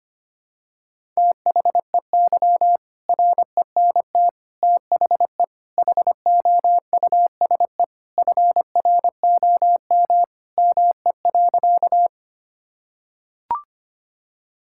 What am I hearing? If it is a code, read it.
Morse code, heard as a keyed tone, about 25 wpm, 700 Hz. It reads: THEY RENT THE HOUSE FROM ME.
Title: Sentences from Top 1000 Words - Review Part 5 - 25wpm